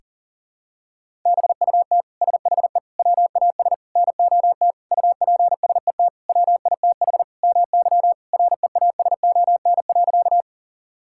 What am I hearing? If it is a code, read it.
Morse code, heard as a keyed tone, about 40 wpm, 700 Hz. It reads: BUT SHE WAS NOT UPSET WITH MY REASON.
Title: Sentences from Top 1000 Words - Review Part 5 - 40wpm